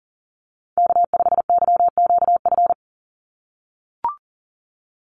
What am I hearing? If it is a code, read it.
Morse code, heard as a keyed tone, about 38 wpm, 700 Hz. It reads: K5YQF
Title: Call Signs - 40wpm